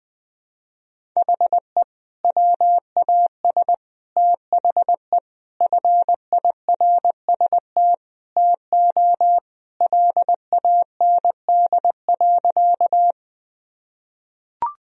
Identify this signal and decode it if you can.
Morse code, heard as a keyed tone, about 20 wpm, 700 Hz. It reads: HE WAS THE FIRST TO LAND.
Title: Sentences from Top 1000 Words - Review Part 1 - 20wpm